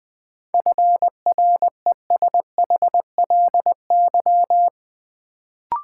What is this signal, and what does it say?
Morse code, heard as a keyed tone, about 20 wpm, 700 Hz. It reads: FRESHLY
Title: Sets of 5 Words Encore - New Word Drill - 20wpm